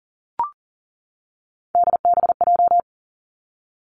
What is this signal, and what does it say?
Morse code, heard as a keyed tone, about 41 wpm, 700 Hz. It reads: DBJ
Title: Three Challenging Letter-Number Combinations 40wpm